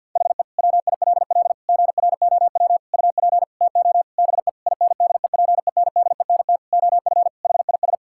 Morse code, heard as a keyed tone, about 50 wpm, 700 Hz. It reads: HE WILL GROW UP TO BE INDEPENDENT OF HIS
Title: Sentences from Top 1000 Words - Review Part 6 - 50wpm